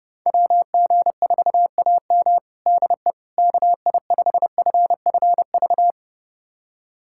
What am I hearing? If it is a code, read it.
Morse code, heard as a keyed tone, about 30 wpm, 700 Hz. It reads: WG4AM DE KI5FFV
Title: USA Callsign DE Callsign 30wpm